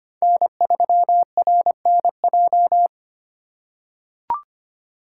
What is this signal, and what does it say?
Morse code, heard as a keyed tone, about 25 wpm, 700 Hz. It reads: N3RNJ
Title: USA Call Signs - 5 Characters - 4x Character Spacing - 25wpm